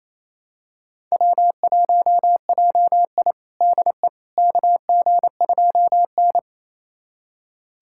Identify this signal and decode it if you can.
Morse code, heard as a keyed tone, about 28 wpm, 700 Hz. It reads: W1JI DE KG2N
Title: USA Callsign DE Callsign - 4 Characters - 28wpm